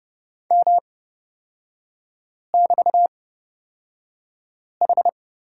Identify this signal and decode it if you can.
Morse code, heard as a keyed tone, about 29 wpm, spaced as for 5 wpm, 700 Hz. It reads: M<BT>H
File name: Lesson 36e - BT - Three Characters 30wpm